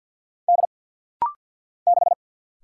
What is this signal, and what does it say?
Morse code, heard as a keyed tone, about 49 wpm, 700 Hz. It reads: D 6
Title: Single Letter-Number - Rapid-Fire 50wpm